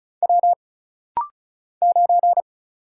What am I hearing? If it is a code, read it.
Morse code, heard as a keyed tone, about 35 wpm, 700 Hz. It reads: W 9
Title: Single Letter-Number - Rapid-Fire 35wpm